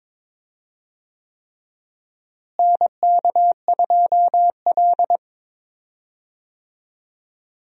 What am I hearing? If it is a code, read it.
Morse code, heard as a keyed tone, about 22 wpm, 700 Hz. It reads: NK2L
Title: USA Call Signs - 4 Characters - Repeated 3x - 22wpm